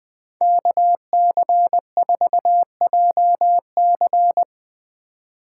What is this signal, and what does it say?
Morse code, heard as a keyed tone, about 20 wpm, 700 Hz. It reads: KC4JC
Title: USA Call Signs - 5 Characters - 3x Character Spacing - 20wpm